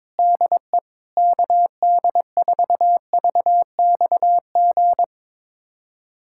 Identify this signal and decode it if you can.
Morse code, heard as a keyed tone, about 22 wpm, 700 Hz. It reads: DE KD4VXG
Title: USA Callsign DE Callsign 22wpm